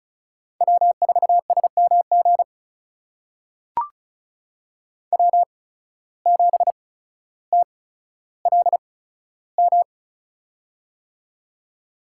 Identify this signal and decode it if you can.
Morse code, heard as a keyed tone, about 36 wpm, spaced as for 15 wpm, 700 Hz. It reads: W4SMG W7TLM
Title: USA Call Signs - 5 Characters - 8x Character Spacing - 35wpm